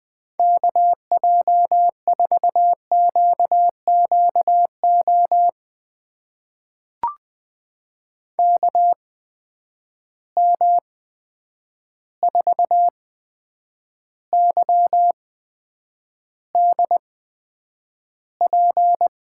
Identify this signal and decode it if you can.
Morse code, heard as a keyed tone, about 20 wpm, spaced as for 9 wpm, 700 Hz. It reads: KJ4QQO KM4YDP
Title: USA Call Signs - 6 Characters - 8x Character Spacing - 20wpm